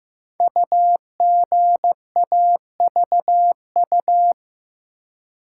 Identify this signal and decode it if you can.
Morse code, heard as a keyed tone, about 15 wpm, 700 Hz. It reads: UGAVU
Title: Code Groups - 5 Letter:Numbers - 15wpm